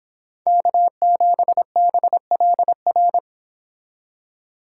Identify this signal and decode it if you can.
Morse code, heard as a keyed tone, about 26 wpm, 700 Hz. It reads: K7BLR